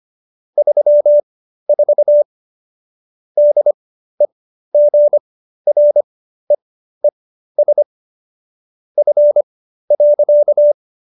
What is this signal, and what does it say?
Morse code, heard as a keyed tone, about 25 wpm, spaced as for 13 wpm, 590 Hz. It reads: 34 DEGREES F.